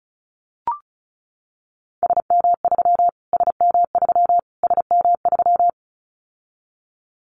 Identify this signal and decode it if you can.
Morse code, heard as a keyed tone, about 34 wpm, 700 Hz. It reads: SM3 SM3 SM3
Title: Callsign Prefixes for Top 25 DXCC - Repeated 3x - 35wpm